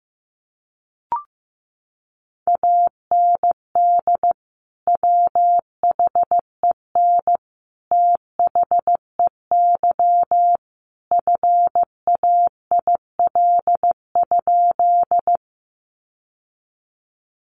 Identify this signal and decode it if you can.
Morse code, heard as a keyed tone, about 15 wpm, 700 Hz. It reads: AND WHEN THEY FAIL?
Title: Sentences from Top 1100 Words 15wpm